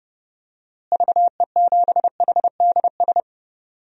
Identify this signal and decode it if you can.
Morse code, heard as a keyed tone, about 31 wpm, 700 Hz. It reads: VE7HDS
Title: DXCC-100 Call Signs - 6 Characters - Repeated 2x - 30wpm